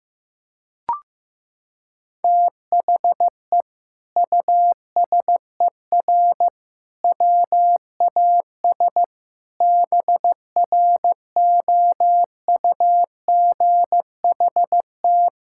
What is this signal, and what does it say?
Morse code, heard as a keyed tone, about 16 wpm, 700 Hz. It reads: THE USER WAS BROUGHT
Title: Sentences from Top 1000 Words - Review Part 5 - 15wpm